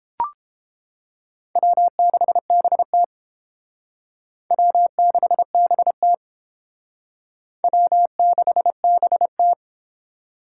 Morse code, heard as a keyed tone, about 30 wpm, 700 Hz. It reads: W6BT W6BT W6BT